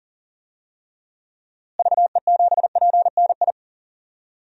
Morse code, heard as a keyed tone, about 40 wpm, 700 Hz. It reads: VE7PNI
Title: DXCC-100 Call Signs - 6 Characters - Repeated 3x - 40wpm